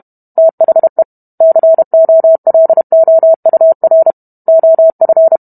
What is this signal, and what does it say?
Morse code, heard as a keyed tone, about 32 wpm, 650 Hz. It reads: THE COLOUR OF